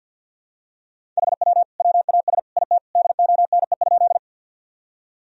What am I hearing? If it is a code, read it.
Morse code, heard as a keyed tone, about 47 wpm, 700 Hz. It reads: HW WAS IT DONE?
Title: Sentences from Top 500 Words and Abbreviations 50wpm